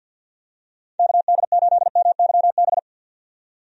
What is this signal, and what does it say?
Morse code, heard as a keyed tone, about 51 wpm, 700 Hz. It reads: KD8MYB
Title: USA Call Signs - 6 Characters - 50wpm